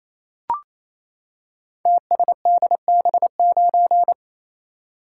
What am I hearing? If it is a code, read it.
Morse code, heard as a keyed tone, about 27 wpm, 700 Hz. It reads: TSDB9
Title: Code Groups - 5 Letter:Numbers - 28wpm